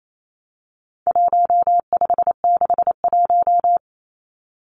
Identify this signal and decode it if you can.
Morse code, heard as a keyed tone, about 28 wpm, 700 Hz. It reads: 1561
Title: Numbers - 4-Digit 28wpm